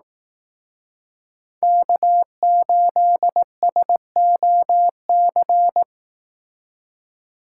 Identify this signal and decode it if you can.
Morse code, heard as a keyed tone, about 18 wpm, 700 Hz. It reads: K8SOC
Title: Speed Racing - USA Call Signs 15wpm